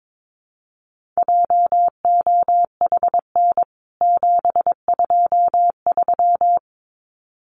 Morse code, heard as a keyed tone, about 22 wpm, 700 Hz. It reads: JOHN 723
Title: CWT Exchange 22wpm